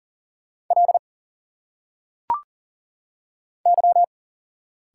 Morse code, heard as a keyed tone, about 40 wpm, 700 Hz. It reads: L Y
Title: Single Letters - Phonetic Pronunciation 40wpm